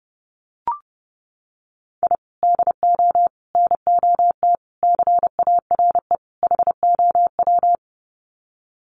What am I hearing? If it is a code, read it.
Morse code, heard as a keyed tone, about 28 wpm, 700 Hz. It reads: I DO NOT CARE HOW
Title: Sets of 5 Words Encore2 30wpm